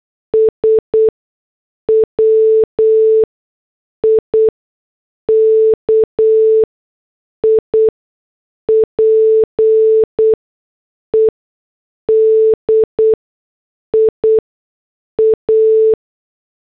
Morse code, heard as a keyed tone, about 8 wpm, spaced as for 6 wpm, 440 Hz. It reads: S WIKIPEDIA
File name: morse / headlines